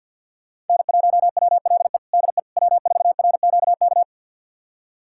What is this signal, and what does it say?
Morse code, heard as a keyed tone, about 51 wpm, 700 Hz. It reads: N1WLE DE W4RQK
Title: USA Callsign DE Callsign - 5 Characters - 50wpm